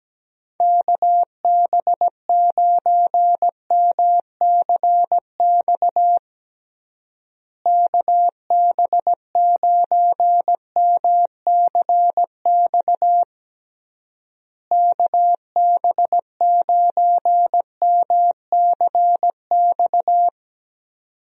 Morse code, heard as a keyed tone, about 17 wpm, 700 Hz. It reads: KB9MCX KB9MCX KB9MCX
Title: USA Call Signs - 6 Characters - Repeated 3x - 17wpm